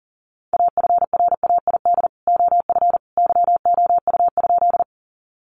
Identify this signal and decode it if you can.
Morse code, heard as a keyed tone, about 40 wpm, 700 Hz. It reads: AFRAID OF YOU?
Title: Sentences from Top 1500 Words 40wpm